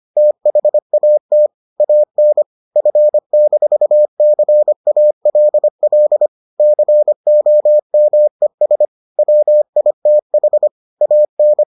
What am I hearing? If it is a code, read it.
Morse code, heard as a keyed tone, about 25 wpm, 600 Hz. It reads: THAT AN F-CALL COMES WITH AN